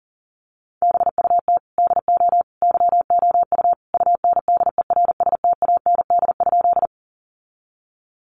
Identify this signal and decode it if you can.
Morse code, heard as a keyed tone, about 40 wpm, 700 Hz. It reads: BUT DO YOU UNDERSTAND?
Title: Sentences from Top 600 Words 40wpm